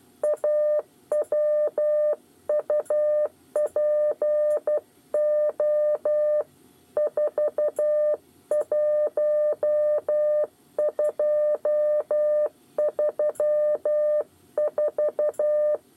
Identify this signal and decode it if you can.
Morse code, heard as a keyed tone, about 12 wpm, 600 Hz. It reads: AWUPO 41234
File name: 8 bis lezione cw 2